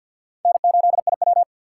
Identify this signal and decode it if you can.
Morse code, heard as a keyed tone, about 51 wpm, 700 Hz. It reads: N8IW
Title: USA Call Signs - 4 Characters - 2x Character Spacing - 50wpm